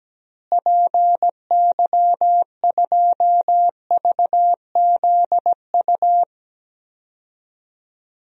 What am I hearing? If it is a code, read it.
Morse code, heard as a keyed tone, about 17 wpm, 700 Hz. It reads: PY2VZU
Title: DXCC-100 Call Signs - 6 Characters - Repeated 3x - 17wpm